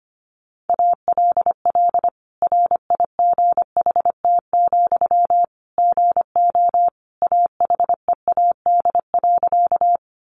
Morse code, heard as a keyed tone, about 25 wpm, 700 Hz. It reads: ALL RIGHT, GO AHEAD.
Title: Sentences from Top 1100 Words 25wpm